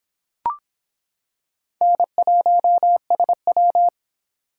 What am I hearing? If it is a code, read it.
Morse code, heard as a keyed tone, about 26 wpm, 700 Hz. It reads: N1SW